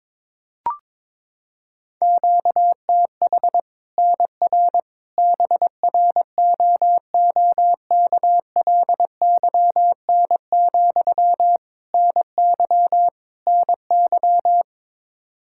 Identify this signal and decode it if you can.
Morse code, heard as a keyed tone, about 22 wpm, 700 Hz. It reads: QTH NR BROOKLYN, NY NY
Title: Practice QSO - Exchange 1 - Beginning - 22wpm